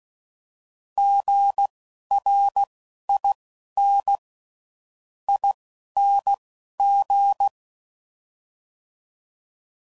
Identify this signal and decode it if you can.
Morse code, heard as a keyed tone, about 16 wpm, spaced as for 11 wpm, 800 Hz. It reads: GRIN ING
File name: Prefixes / By First Letter